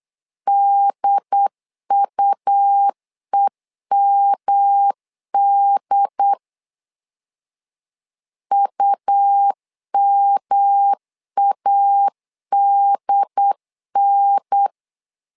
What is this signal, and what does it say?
Morse code, heard as a keyed tone, about 8 wpm, 800 Hz. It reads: DUEMD UMADN
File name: LEZIONI / 3 MWRO / 3 M 8wpm 2300